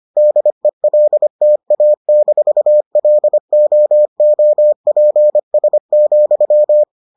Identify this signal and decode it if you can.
Morse code, heard as a keyed tone, about 25 wpm, 600 Hz. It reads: DELTA-LOOPS,